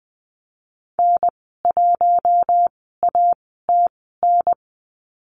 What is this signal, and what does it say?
Morse code, heard as a keyed tone, about 20 wpm, spaced as for 14 wpm, 700 Hz. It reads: N1ATN
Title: 042 Lesson 8g - Number 1 - Call Signs 20wpm